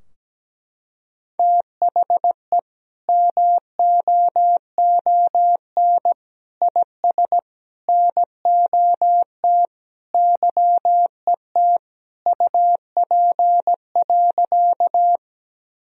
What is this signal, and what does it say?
Morse code, heard as a keyed tone, about 17 wpm, 700 Hz. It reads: THE MOON IS NOT YET UP.
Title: Sentences from Top 1000 Words - Review Part 3 - 17wpm